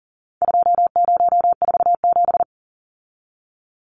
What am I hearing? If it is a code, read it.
Morse code, heard as a keyed tone, about 40 wpm, 700 Hz. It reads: 2047
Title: Numbers - 4-Digit 40wpm